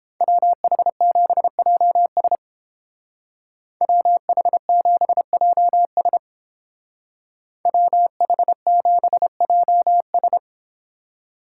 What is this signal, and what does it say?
Morse code, heard as a keyed tone, about 30 wpm, 700 Hz. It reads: WH7JS WH7JS WH7JS